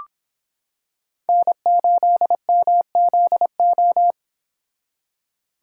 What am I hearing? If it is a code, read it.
Morse code, heard as a keyed tone, about 26 wpm, 700 Hz. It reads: N8MZO